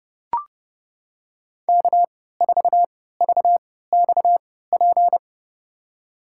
Morse code, heard as a keyed tone, about 30 wpm, spaced as for 17 wpm, 700 Hz. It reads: K4VXP
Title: USA Call Signs - 3x Character Spacing - 30wpm